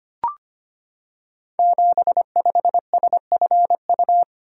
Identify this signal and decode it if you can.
Morse code, heard as a keyed tone, about 25 wpm, 700 Hz. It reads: 75SFU
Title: Code Groups - 5 Letter:Numbers - 25wpm